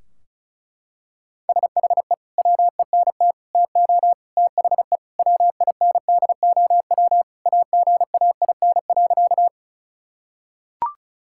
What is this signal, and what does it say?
Morse code, heard as a keyed tone, about 36 wpm, 700 Hz. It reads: SHE WENT TO THE WINDOW AGAIN.
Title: Sentences from Top 1600 Words 35wpm